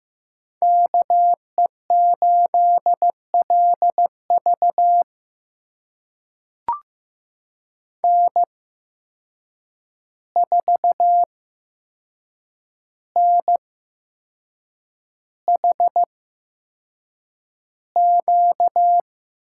Morse code, heard as a keyed tone, about 15 wpm, spaced as for 6 wpm, 700 Hz. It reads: KE8LV N4NHQ
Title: USA Call Signs - 5 Characters - 8x Character Spacing - 15wpm